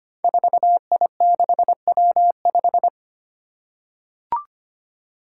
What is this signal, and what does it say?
Morse code, heard as a keyed tone, about 25 wpm, 700 Hz. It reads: 4I6W5
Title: Five Challenging Letter-Number Combinations 25wpm